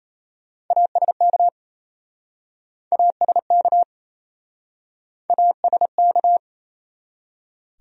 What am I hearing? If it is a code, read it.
Morse code, heard as a keyed tone, about 37 wpm, 700 Hz. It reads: ASK ASK ASK